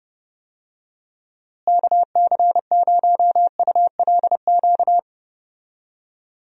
Morse code, heard as a keyed tone, about 30 wpm, 700 Hz. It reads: KC0ULQ
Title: USA Call Signs - 6 Characters - Repeated 3x - 30wpm